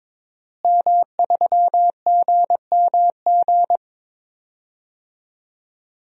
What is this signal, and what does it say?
Morse code, heard as a keyed tone, about 22 wpm, 700 Hz. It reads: M3GMG
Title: DXCC-100 Call Signs - 5 Characters - Repeated 3x - 22wpm